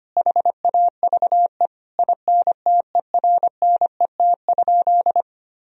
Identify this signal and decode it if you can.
Morse code, heard as a keyed tone, about 25 wpm, 700 Hz. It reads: HAVE INTERNET?
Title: Sentences from Top 1000 Words - Review Part 4 - 25wpm